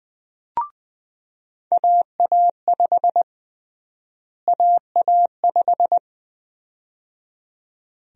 Morse code, heard as a keyed tone, about 20 wpm, 700 Hz. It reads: AA5 AA5
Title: USA Call Sign Prefixes - Repeated 2x - 20wpm